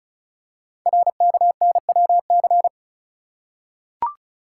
Morse code, heard as a keyed tone, about 36 wpm, 690 Hz. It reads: RKNWC
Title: Code Groups - 5 Letters - 35wpm